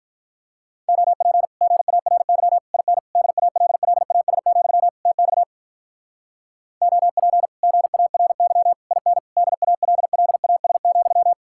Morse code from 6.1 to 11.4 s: OP GARY IN DALLAS,

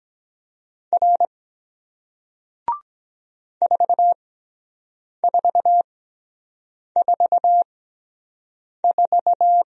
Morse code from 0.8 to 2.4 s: R

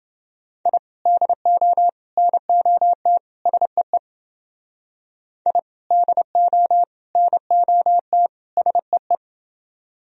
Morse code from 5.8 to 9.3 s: DO NOT SEE

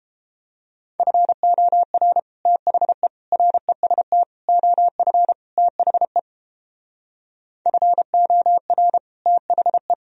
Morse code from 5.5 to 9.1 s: THE FOR

